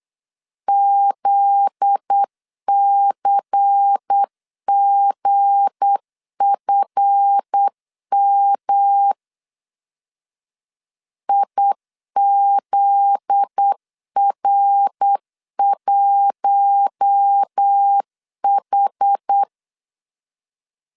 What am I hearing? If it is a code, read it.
Morse code, heard as a keyed tone, about 8 wpm, 800 Hz. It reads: ZCGFM IZR1H
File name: LEZIONI / 7 GF9Z / 7 GF9Z 8wpm 3213